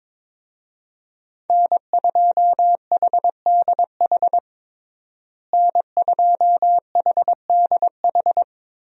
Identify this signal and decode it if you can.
Morse code, heard as a keyed tone, about 22 wpm, 700 Hz. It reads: N2HDH N2HDH
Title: USA Call Signs - 5 Characters - Repeated 2x - 22wpm